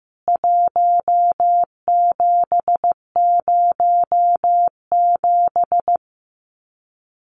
Morse code, heard as a keyed tone, about 15 wpm, 700 Hz. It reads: 1707
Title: CWT Exchange 15wpm